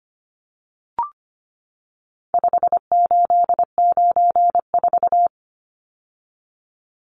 Morse code, heard as a keyed tone, about 25 wpm, 700 Hz. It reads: 5894
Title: Numbers - 4-Digit 25wpm